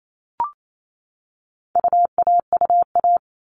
Morse code, heard as a keyed tone, about 28 wpm, 700 Hz. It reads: UAUA
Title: All Cut Numbers - 4-Digit 28wpm